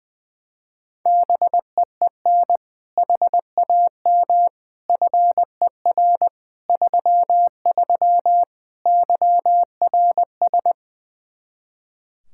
Morse code, heard as a keyed tone, about 20 wpm, 700 Hz. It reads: BEEN HAM FER 33 YRS
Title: Practice QSO - Exchange 3 - More Info - 20wpm